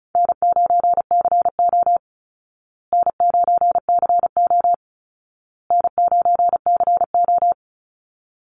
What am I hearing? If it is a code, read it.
Morse code, heard as a keyed tone, about 34 wpm, 700 Hz. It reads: N9CO N9CO N9CO
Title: Easy Call Signs Repeated 3x - 35wpm